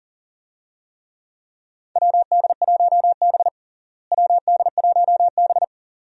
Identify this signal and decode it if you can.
Morse code, heard as a keyed tone, about 40 wpm, 700 Hz. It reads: WD1B WD1B
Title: USA Call Signs - 4 Characters - Repeated 2x - 40wpm